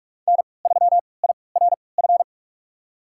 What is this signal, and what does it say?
Morse code, heard as a keyed tone, about 42 wpm, spaced as for 25 wpm, 700 Hz. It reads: N3IRF
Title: USA Call Signs - 5 Characters - 3x Character Spacing - 45wpm